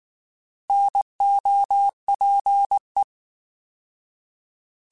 Morse code, heard as a keyed tone, about 19 wpm, 800 Hz. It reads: NOPE
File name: Alphabetical / Alphabetical mp3s / Ns @18